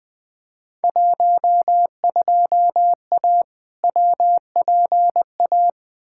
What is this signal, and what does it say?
Morse code, heard as a keyed tone, about 20 wpm, 700 Hz. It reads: 12A WPA
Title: Field Day Exchanges Encore - 20wpm